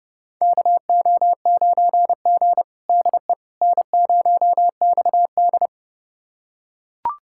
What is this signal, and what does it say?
Morse code, heard as a keyed tone, about 30 wpm, 700 Hz. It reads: KO9G DE N0XD
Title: USA Callsign DE Callsign - 4 Characters - 30wpm